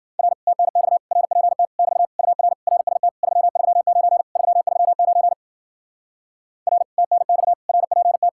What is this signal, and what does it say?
Morse code, heard as a keyed tone, about 59 wpm, 700 Hz. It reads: R TNX RPT <BT> UR RST 339 339 R TNX RPT